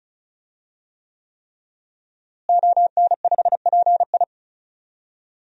Morse code, heard as a keyed tone, about 34 wpm, 700 Hz. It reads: ON5PI
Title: DXCC-100 Call Signs - 5 Characters - Repeated 3x - 35wpm